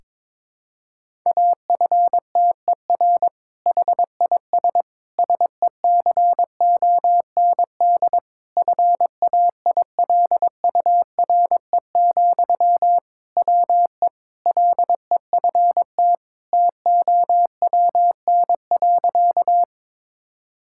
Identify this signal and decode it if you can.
Morse code, heard as a keyed tone, about 22 wpm, 700 Hz. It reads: AFTER HIS SECOND FAILURE, WE LEFT TOWN.
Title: Sentences from Top 1000 Words - Review Part 6 - 22wpm